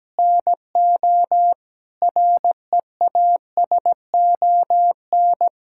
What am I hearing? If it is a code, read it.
Morse code, heard as a keyed tone, about 17 wpm, 700 Hz. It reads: NO REASON
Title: Sentences from Top 1000 Words - Review Part 6 - 17wpm